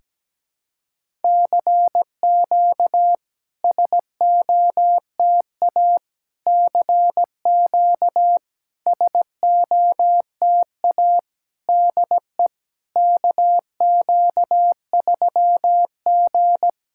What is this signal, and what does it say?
Morse code, heard as a keyed tone, about 17 wpm, 700 Hz. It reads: CQ SOTA CQ SOTA DE KQ3G
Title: SOTA QSO Practice 17wpm